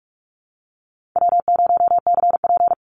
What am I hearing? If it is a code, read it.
Morse code, heard as a keyed tone, about 44 wpm, 700 Hz. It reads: W0CP
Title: Easy Call Signs Repeated 2x - 45wpm